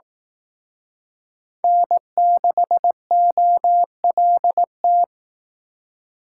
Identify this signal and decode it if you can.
Morse code, heard as a keyed tone, about 18 wpm, 700 Hz. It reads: N6OLT
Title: Speed Racing - USA 1x3 Call Signs 15wpm